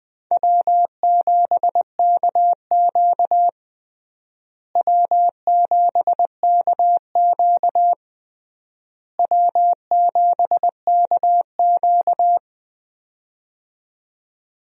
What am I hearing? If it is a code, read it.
Morse code, heard as a keyed tone, about 20 wpm, 700 Hz. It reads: W7KQ W7KQ W7KQ
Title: USA Call Signs - 4 Characters - Repeated 3x - 20wpm